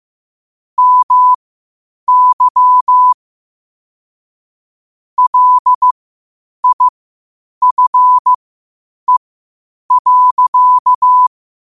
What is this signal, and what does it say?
Morse code, heard as a keyed tone, about 15 wpm, spaced as for 8 wpm, 1000 Hz. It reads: MY LIFE.
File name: qotdfiles / 8wpm